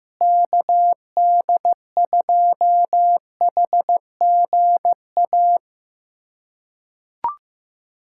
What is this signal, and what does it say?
Morse code, heard as a keyed tone, about 15 wpm, 700 Hz. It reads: KD2HGA
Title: USA Callsign DE Callsign 15wpm